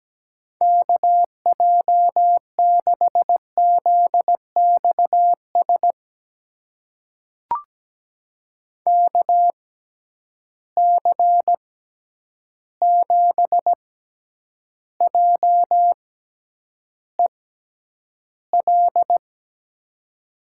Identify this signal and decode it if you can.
Morse code, heard as a keyed tone, about 17 wpm, spaced as for 9 wpm, 700 Hz. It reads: KJ6ZXS KC7JEL
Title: USA Call Signs - 6 Characters - 6x Character Spacing - 17wpm